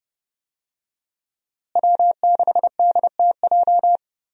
Morse code, heard as a keyed tone, about 32 wpm, 700 Hz. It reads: W6DTJ
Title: USA Call Signs - 5 Characters - Repeated 2x - 30wpm